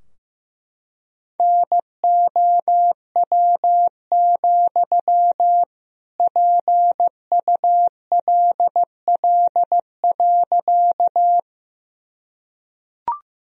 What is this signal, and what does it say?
Morse code, heard as a keyed tone, about 15 wpm, 700 Hz. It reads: NOW, PULL.
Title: Sentences from Top 1000 Words - Review Part 2 - 15wpm